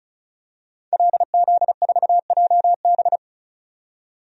Morse code, heard as a keyed tone, about 36 wpm, 700 Hz. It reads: LZ4JB
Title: DXCC-100 Call Signs - 5 Characters - 35wpm